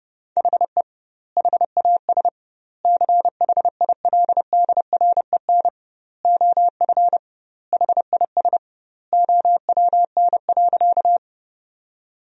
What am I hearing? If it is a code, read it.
Morse code, heard as a keyed tone, about 30 wpm, 700 Hz. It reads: HE HAS CHILDREN OF HIS OWN.
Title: 2x Word-Spacing - Sentences from Top 300 Words - 30wpm